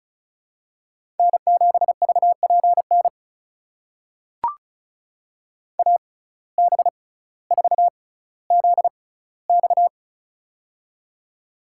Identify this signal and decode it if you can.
Morse code, heard as a keyed tone, about 36 wpm, spaced as for 18 wpm, 700 Hz. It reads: N7VPN AB4ZX
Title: USA Call Signs - 5 Characters - 6x Character Spacing - 35wpm